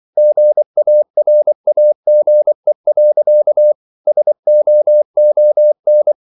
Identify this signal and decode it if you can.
Morse code, heard as a keyed tone, about 24 wpm, 600 Hz. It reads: GARAGE. SOON